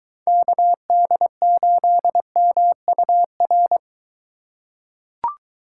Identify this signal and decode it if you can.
Morse code, heard as a keyed tone, about 22 wpm, 700 Hz. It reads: KD8MUR